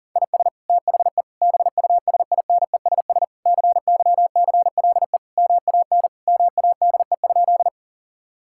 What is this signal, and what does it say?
Morse code, heard as a keyed tone, about 40 wpm, 700 Hz. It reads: IS THE BUSINESS CYCLE MAN MADE?
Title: Sentences from Top 1000 Words - Review Part 6 - 40wpm